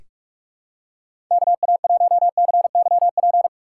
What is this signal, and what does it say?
Morse code, heard as a keyed tone, about 43 wpm, 700 Hz. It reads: KA1CYP
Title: USA Call Signs - 6 Characters - 6x Character Spacing - 45wpm